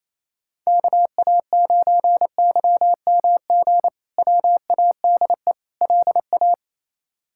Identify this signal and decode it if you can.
Morse code, heard as a keyed tone, about 28 wpm, 700 Hz. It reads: KA9YMG WADE LA
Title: K1USN SST QSO Practice - 28wpm